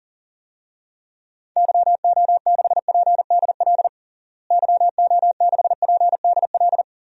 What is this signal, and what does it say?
Morse code, heard as a keyed tone, about 40 wpm, 700 Hz. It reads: YO6PDL YO6PDL
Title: DXCC-100 Call Signs - 6 Characters - Repeated 2x - 40wpm